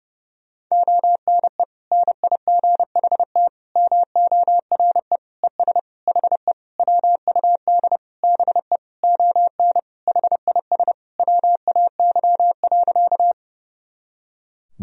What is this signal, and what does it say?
Morse code, heard as a keyed tone, about 30 wpm, 700 Hz. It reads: ONE NIGHT MORE ES HE WUD BE ON HIS WAY.
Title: Sentences from Top 300 Words and Abbreviations 30wpm